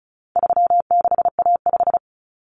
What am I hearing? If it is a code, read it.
Morse code, heard as a keyed tone, about 34 wpm, 700 Hz. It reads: 36A5
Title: Common Cut Numbers - 4-Digit 35wpm